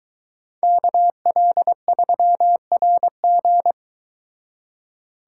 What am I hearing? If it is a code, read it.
Morse code, heard as a keyed tone, about 23 wpm, 700 Hz. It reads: KL3RG